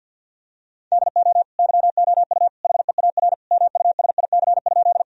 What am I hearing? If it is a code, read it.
Morse code, heard as a keyed tone, about 47 wpm, 700 Hz. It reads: DO YOU HEAR MUSIC?
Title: Sentences from Top 1000 Words - Review Part 1 - 50wpm